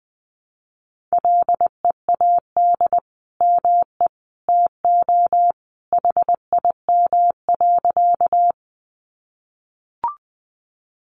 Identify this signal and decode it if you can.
Morse code, heard as a keyed tone, about 20 wpm, 700 Hz. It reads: LEAD ME TO HIM.
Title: Sentences from Top 500 Words 20wpm